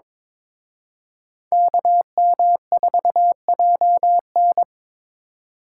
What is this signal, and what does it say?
Morse code, heard as a keyed tone, about 25 wpm, 700 Hz. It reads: KM4JN
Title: Speed Racing - USA 2x2 Call Signs 22wpm